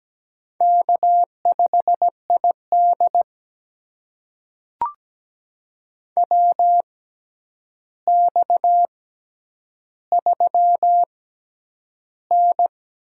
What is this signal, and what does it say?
Morse code, heard as a keyed tone, about 17 wpm, spaced as for 9 wpm, 700 Hz. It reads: K5ID WX3N
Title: USA Call Signs - 4 Characters - 6x Character Spacing - 17wpm